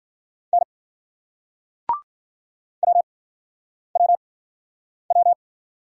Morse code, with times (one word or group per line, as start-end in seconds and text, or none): 0.5-0.6 N
2.8-5.3 WWW